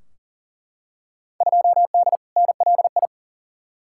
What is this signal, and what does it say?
Morse code, heard as a keyed tone, about 40 wpm, 700 Hz. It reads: 2D NLI
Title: Field Day Exchanges - 40wpm